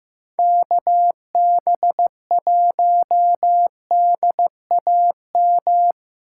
Morse code, heard as a keyed tone, about 15 wpm, 700 Hz. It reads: KB1DAM
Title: USA Call Signs - 6 Characters - 3x Character Spacing - 15wpm